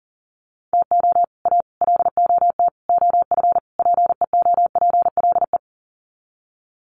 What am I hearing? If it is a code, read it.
Morse code, heard as a keyed tone, about 40 wpm, 700 Hz. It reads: TO A LOT OF PEOPLE
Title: Sets of 5 Words Encore 40wpm